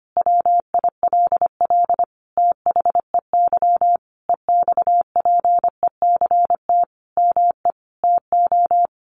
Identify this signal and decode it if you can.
Morse code, heard as a keyed tone, about 25 wpm, 700 Hz. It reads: WILL THEY EXPECT ME TO
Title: Sentences from Top 1100 Words 25wpm